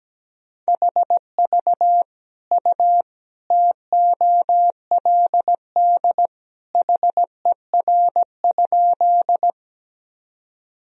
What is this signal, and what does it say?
Morse code, heard as a keyed tone, about 17 wpm, 700 Hz. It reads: HV U TOLD HER?
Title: Sentences from Top 400 Words and Abbreviations 17wpm